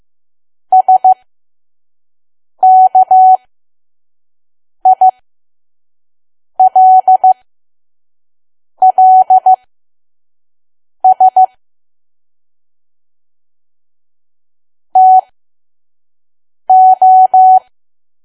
Morse code, heard as a keyed tone, about 15 wpm, spaced as for 5 wpm, 750 Hz. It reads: SKILLS TO
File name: Archive / 5 WPM files / PM